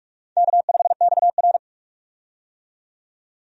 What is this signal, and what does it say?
Morse code, heard as a keyed tone, about 44 wpm, 700 Hz. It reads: K5XR
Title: USA Call Signs - 4 Characters - 45wpm